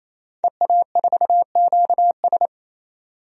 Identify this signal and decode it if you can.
Morse code, heard as a keyed tone, about 28 wpm, 700 Hz. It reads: EA4QS
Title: DXCC-100 Call Signs - 5 Characters - 28wpm